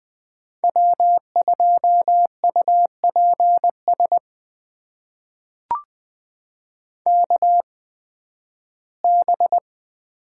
Morse code, heard as a keyed tone, about 20 wpm, spaced as for 13 wpm, 700 Hz. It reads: W2UPS KB
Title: USA Call Signs - 5 Characters - 8x Character Spacing - 20wpm